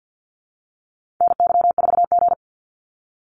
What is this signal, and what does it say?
Morse code, heard as a keyed tone, about 50 wpm, 700 Hz. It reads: NY4G
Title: NA SOTA Activator Callsigns 50wpm